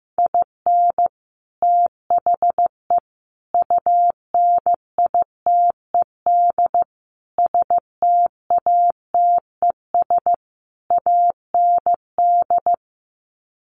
Morse code, heard as a keyed tone, about 15 wpm, 700 Hz. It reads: IN THE UNITED STATES AND
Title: Sets of 5 Words 15wpm